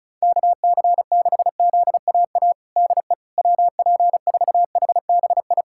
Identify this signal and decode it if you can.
Morse code, heard as a keyed tone, about 34 wpm, 700 Hz. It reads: KC6ZAA DE WP4HBI
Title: USA Callsign DE Callsign - 6 Characters - 35wpm